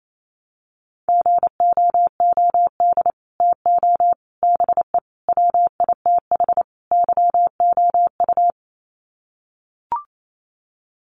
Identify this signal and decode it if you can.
Morse code, heard as a keyed tone, about 28 wpm, 700 Hz. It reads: GOOD TO BE WITH YOU
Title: Sets of 5 Words Encore2 28wpm